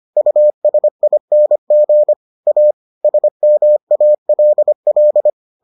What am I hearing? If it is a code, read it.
Morse code, heard as a keyed tone, about 25 wpm, 600 Hz. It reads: USING A SMALL